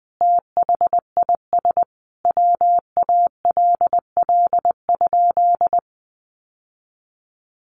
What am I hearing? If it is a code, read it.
Morse code, heard as a keyed tone, about 20 wpm, 700 Hz. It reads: THIS WALL?
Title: Sentences from Top 1100 Words 20wpm